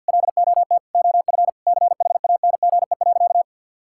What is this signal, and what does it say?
Morse code, heard as a keyed tone, about 50 wpm, 700 Hz. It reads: LOT OF CHANGE.
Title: Sentences from Top 1000 Words - Review Part 5 - 50wpm